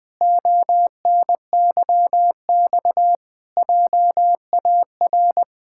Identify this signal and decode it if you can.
Morse code, heard as a keyed tone, about 20 wpm, 700 Hz. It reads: ONYX JAR
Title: Pangram Challenge v2 - 20wpm